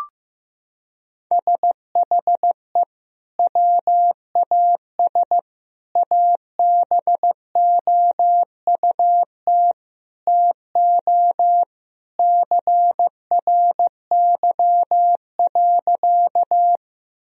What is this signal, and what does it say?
Morse code, heard as a keyed tone, about 15 wpm, 700 Hz. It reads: SHE WAS ABOUT TO CRY.
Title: Sentences from Top 1000 Words - Review Part 2 - 15wpm